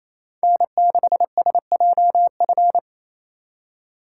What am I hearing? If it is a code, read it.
Morse code, heard as a keyed tone, about 34 wpm, 700 Hz. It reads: N6SJF
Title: Speed Racing - USA 1x3 Call Signs 28wpm